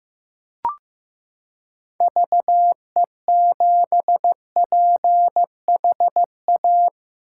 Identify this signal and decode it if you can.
Morse code, heard as a keyed tone, about 15 wpm, 700 Hz. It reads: VE7PHA
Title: K1USN SST QSO Practice - 15wpm